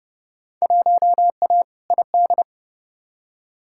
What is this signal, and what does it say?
Morse code, heard as a keyed tone, about 31 wpm, 700 Hz. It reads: 1A ID
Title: Field Day Exchanges - 30wpm